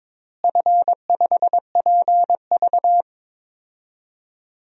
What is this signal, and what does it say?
Morse code, heard as a keyed tone, about 22 wpm, 700 Hz. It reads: F5PV
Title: DXCC-100 Call Signs - 4 Characters - Repeated 2x - 22wpm